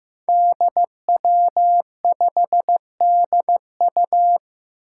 Morse code, heard as a keyed tone, about 15 wpm, 700 Hz. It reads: DW5DU
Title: Five Challenging Letter-Number Combinations 15wpm